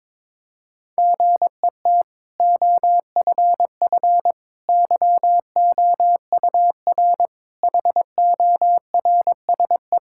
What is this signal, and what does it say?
Morse code, heard as a keyed tone, about 22 wpm, 700 Hz. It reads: GET OFF YOUR HORSE
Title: Sentences from Top 1600 Words 22wpm